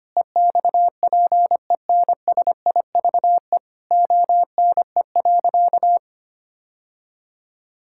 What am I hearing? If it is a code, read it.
Morse code, heard as a keyed tone, about 25 wpm, 700 Hz. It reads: EXPENSIVE ONE.
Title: Sentences from Top 1000 Words - Review Part 6 - 25wpm